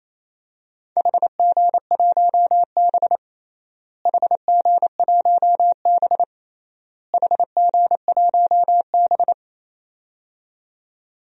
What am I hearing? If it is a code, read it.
Morse code, heard as a keyed tone, about 28 wpm, 700 Hz. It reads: HG1B HG1B HG1B
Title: DXCC-100 Call Signs - 4 Characters - Repeated 3x - 28wpm